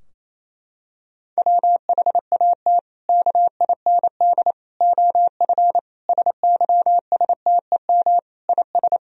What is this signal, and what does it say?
Morse code, heard as a keyed tone, about 28 wpm, 700 Hz. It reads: WHAT KIND OF SYSTEM IS
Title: Sentences from Top 1000 Words - Review Part 3 - 28wpm